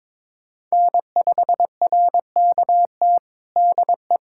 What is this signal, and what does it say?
Morse code, heard as a keyed tone, about 22 wpm, 700 Hz. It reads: N5RKT DE
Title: USA Callsign DE Callsign - 5 Characters - 22wpm